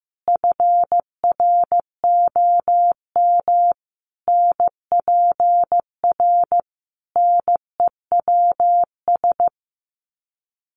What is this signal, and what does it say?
Morse code, heard as a keyed tone, about 15 wpm, 700 Hz. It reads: FROM NPR NEWS
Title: Sets of 5 Words 15wpm